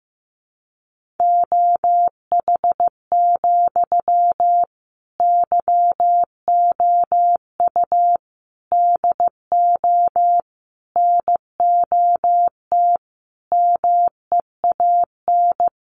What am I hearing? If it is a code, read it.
Morse code, heard as a keyed tone, about 15 wpm, 700 Hz. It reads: OH, YOU DO NOT MEAN
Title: Sentences from Top 600 Words 15wpm